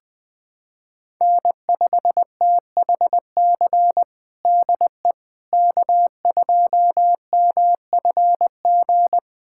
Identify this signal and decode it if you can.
Morse code, heard as a keyed tone, about 20 wpm, 700 Hz. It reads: N5THC DE K2MFG
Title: USA Callsign DE Callsign - 5 Characters - 20wpm